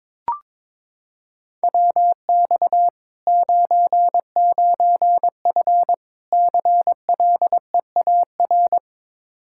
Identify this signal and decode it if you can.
Morse code, heard as a keyed tone, about 22 wpm, 700 Hz. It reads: WX 99F CLEAR
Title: Practice QSO - Exchange 2 - Info - 22wpm